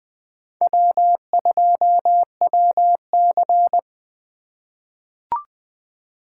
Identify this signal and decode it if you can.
Morse code, heard as a keyed tone, about 20 wpm, 700 Hz. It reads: W2WC
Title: USA Call Signs - 4 Characters - 4x Character Spacing - 20wpm